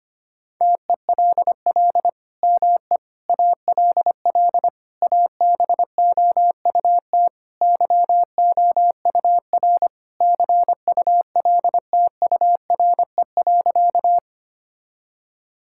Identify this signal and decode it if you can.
Morse code, heard as a keyed tone, about 25 wpm, 700 Hz. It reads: TELL ME ALL ABOUT YOUR CULTURE.
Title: Sentences from Top 1000 Words - Review Part 6 - 25wpm